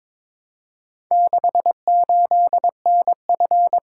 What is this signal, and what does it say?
Morse code, heard as a keyed tone, about 22 wpm, 700 Hz. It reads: 68NF
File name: practice / 22wpm / Four Letter-Number Combinations - 22wpm